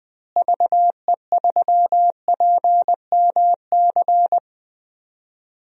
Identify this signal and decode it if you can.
Morse code, heard as a keyed tone, about 21 wpm, 700 Hz. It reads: VE3PMC
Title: DXCC-100 Call Signs - 6 Characters - Repeated 3x - 20wpm